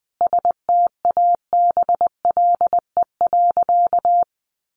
Morse code, heard as a keyed tone, about 20 wpm, 700 Hz. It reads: STABLE.
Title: Sentences from Top 1500 Words 20wpm